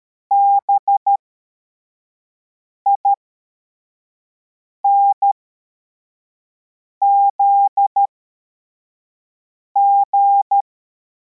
Morse code, heard as a keyed tone, about 13 wpm, spaced as for 4 wpm, 800 Hz. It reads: BINZG